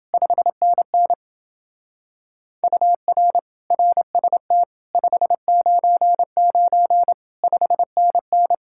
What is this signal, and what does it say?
Morse code, heard as a keyed tone, about 28 wpm, 700 Hz. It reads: 5NN UR RST 599 5NN